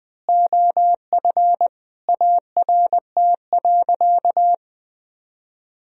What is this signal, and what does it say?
Morse code, heard as a keyed tone, about 20 wpm, 700 Hz. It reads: OF ART.
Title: Sentences from Top 1000 Words - Review Part 3 - 20wpm